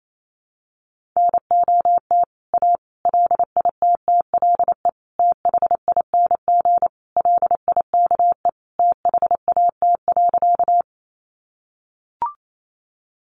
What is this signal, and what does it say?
Morse code, heard as a keyed tone, about 28 wpm, 700 Hz. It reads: NOT A LITTLE THING LIKE THAT.
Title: A Sound Of Thunder 28wpm